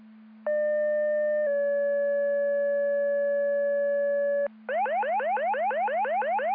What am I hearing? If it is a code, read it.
Two-tone page, heard as a one-second tone, then a three-second tone, about 600 then 570 Hz. It is heard from 0.5 to 4.5 s.